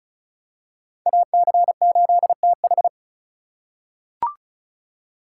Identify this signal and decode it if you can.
Morse code, heard as a keyed tone, about 36 wpm, 700 Hz. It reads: AC8TH
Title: USA Call Signs - 4x Character Spacing - 35wpm